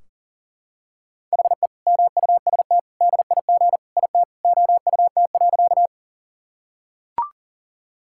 Morse code, heard as a keyed tone, about 37 wpm, 700 Hz. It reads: HE MUST DIG IT OUT.
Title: Sentences from Top 1600 Words 40wpm